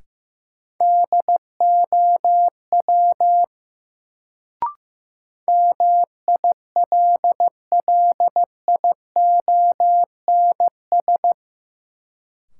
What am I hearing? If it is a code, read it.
Morse code, heard as a keyed tone, about 16 wpm, 700 Hz. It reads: DOW MILLIONS
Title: Sets of 5 Words - New Word Drill - 15wpm